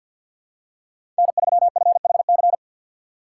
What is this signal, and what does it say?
Morse code, heard as a keyed tone, about 52 wpm, 700 Hz. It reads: N3FHC
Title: USA Call Signs - 5 Characters - 6x Character Spacing - 50wpm